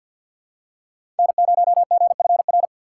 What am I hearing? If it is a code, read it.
Morse code, heard as a keyed tone, about 47 wpm, 700 Hz. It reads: N0GFR
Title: USA Call Signs - 5 Characters - 4x Character Spacing - 50wpm